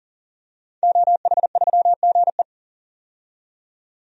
Morse code, heard as a keyed tone, about 40 wpm, 700 Hz. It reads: OH3GE
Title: DXCC-100 Call Signs - 5 Characters - 40wpm